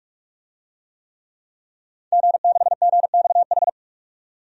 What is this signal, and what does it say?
Morse code, heard as a keyed tone, about 46 wpm, 700 Hz. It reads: G6GXH
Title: DXCC-100 Call Signs - 5 Characters - 45wpm